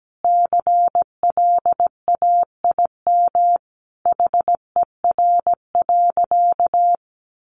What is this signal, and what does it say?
Morse code, heard as a keyed tone, about 17 wpm, 700 Hz. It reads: CLAIM HER.
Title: Sentences from Top 1500 Words 17wpm